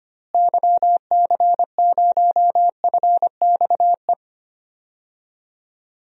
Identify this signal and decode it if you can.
Morse code, heard as a keyed tone, about 25 wpm, 700 Hz. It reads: YC0FXE
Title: DXCC-100 Call Signs - 6 Characters - Repeated 2x - 25wpm